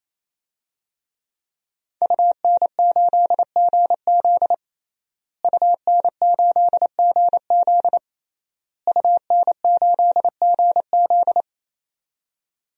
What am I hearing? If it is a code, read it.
Morse code, heard as a keyed tone, about 28 wpm, 700 Hz. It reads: UN8GZ UN8GZ UN8GZ
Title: DXCC-100 Call Signs - 5 Characters - Repeated 3x - 28wpm